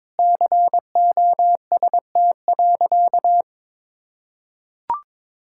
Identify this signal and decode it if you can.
Morse code, heard as a keyed tone, about 22 wpm, 700 Hz. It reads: COST.
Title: Sentences from Top 1000 Words - Review Part 5 - 22wpm